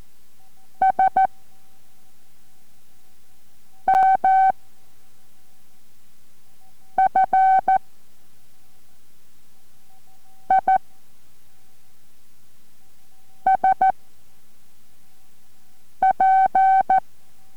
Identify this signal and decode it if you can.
Morse code, heard as a keyed tone, about 14 wpm, 760 Hz. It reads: S M F I S P